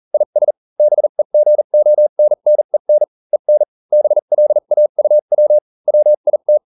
Morse code, heard as a keyed tone, about 40 wpm, spaced as for 35 wpm, 600 Hz. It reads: IS BEGONNEN EN BLAUW WIT